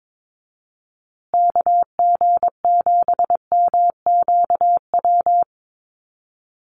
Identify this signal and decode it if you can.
Morse code, heard as a keyed tone, about 22 wpm, 700 Hz. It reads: KG7MQW
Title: USA Call Signs Repeated 2x 22wpm